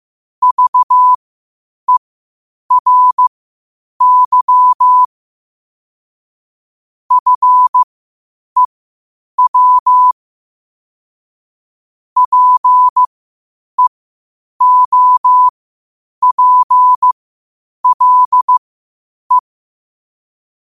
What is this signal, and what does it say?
Morse code, heard as a keyed tone, about 15 wpm, spaced as for 8 wpm, 1000 Hz. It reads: VERY FEW PEOPLE
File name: qotdfiles / 8wpm